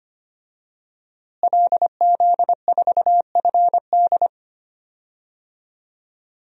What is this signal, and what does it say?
Morse code, heard as a keyed tone, about 25 wpm, 700 Hz. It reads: LZ4FD
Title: DXCC-100 Call Signs - 5 Characters - Repeated 3x - 25wpm